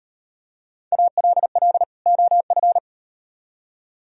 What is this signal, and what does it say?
Morse code, heard as a keyed tone, about 38 wpm, 700 Hz. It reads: ALL OF